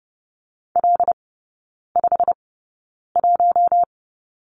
Morse code, heard as a keyed tone, about 30 wpm, spaced as for 9 wpm, 700 Hz. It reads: L51
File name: Lesson 15e - Number 5 - Three Characters 30wpm